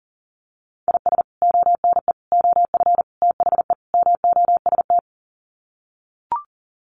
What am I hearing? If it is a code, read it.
Morse code, heard as a keyed tone, about 40 wpm, 700 Hz. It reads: IS ONE OF THE MOST
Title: Sets of 5 Words 40wpm